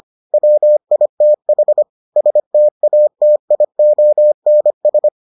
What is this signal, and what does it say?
Morse code, heard as a keyed tone, about 25 wpm, 600 Hz. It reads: WITH STATIONS